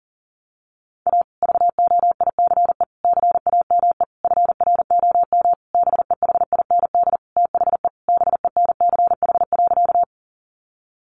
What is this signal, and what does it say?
Morse code, heard as a keyed tone, about 40 wpm, 700 Hz. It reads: A VOICE CAME FROM BEHIND THE BENCH.
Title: Sentences from Top 1400 Words 40wpm